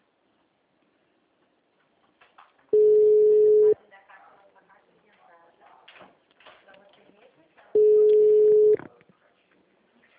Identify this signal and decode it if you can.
Ringback tone (Europe, 425 Hz), with speech.